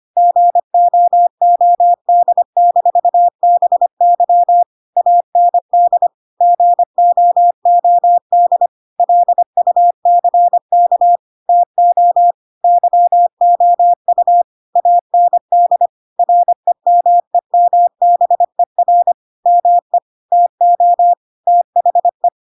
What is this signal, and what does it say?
Morse code, heard as a keyed tone, about 25 wpm, 700 Hz. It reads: GOOD-BY AND GOOD LUCK TO YOU AND REMEMBER ME TO THE
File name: Pinocchio / Pinocchio ch5 25 25 ik1hge